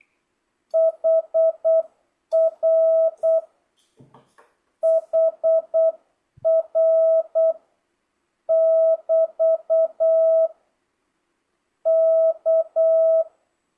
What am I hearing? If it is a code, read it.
Morse code, heard as a keyed tone, about 8 wpm, 650 Hz. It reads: HR HR <BT> K
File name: jh8cbh / 2025cw